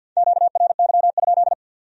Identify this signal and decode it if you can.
Morse code, heard as a keyed tone, about 60 wpm, 700 Hz. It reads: QRY?
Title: Speed-Racing Q-Codes - 50wpm